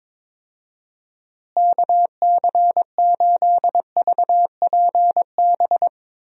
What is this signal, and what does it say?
Morse code, heard as a keyed tone, about 22 wpm, 700 Hz. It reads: KC8VPB
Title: USA Call Signs - 6 Characters - Repeated 2x - 22wpm